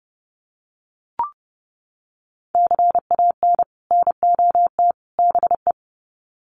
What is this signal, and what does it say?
Morse code, heard as a keyed tone, about 30 wpm, 700 Hz. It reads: CAN NOT BE
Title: Sets of 3 Words 30wpm